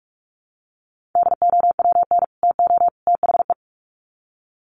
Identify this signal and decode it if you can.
Morse code, heard as a keyed tone, about 44 wpm, 700 Hz. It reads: DOWN TO THE
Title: Sets of 3 Words 45wpm